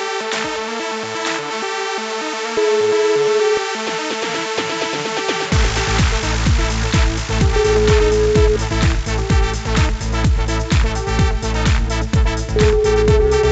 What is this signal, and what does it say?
Ringback tone (Europe, 425 Hz), with noise.